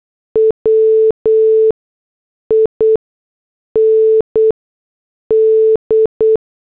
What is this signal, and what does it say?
Morse code, heard as a keyed tone, about 8 wpm, 440 Hz. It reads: W I N D
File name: morse / headlines